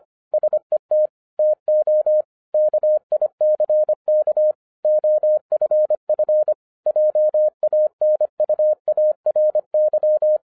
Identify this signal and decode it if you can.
Morse code, heard as a keyed tone, about 24 wpm, 610 Hz. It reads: SET TO KICK OFF JANUARY
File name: 2012-01-mp3 / n14-FUN WITH NUMBERS-25wpm